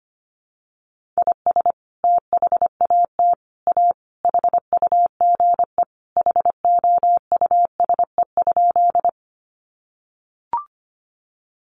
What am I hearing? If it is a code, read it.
Morse code, heard as a keyed tone, about 25 wpm, 700 Hz. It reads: IS THAT A HUGE HOUSE?
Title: Sentences from Top 1100 Words 25wpm